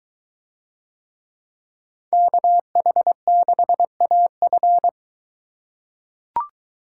Morse code, heard as a keyed tone, about 21 wpm, 700 Hz. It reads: KH6AF